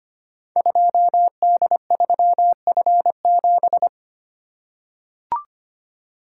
Morse code, heard as a keyed tone, about 25 wpm, 700 Hz. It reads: 2D3F7
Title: Code Groups - 5 Letter:Numbers - 25wpm